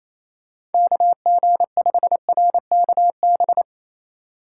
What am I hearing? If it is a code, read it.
Morse code, heard as a keyed tone, about 28 wpm, 700 Hz. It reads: KG5RKB
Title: USA Call Signs - 6 Characters - 2x Character Spacing - 28wpm